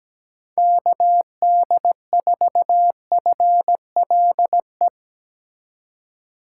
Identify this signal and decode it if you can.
Morse code, heard as a keyed tone, about 17 wpm, 700 Hz. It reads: KD4FLE
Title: USA Call Signs - 6 Characters - 3x Character Spacing - 17wpm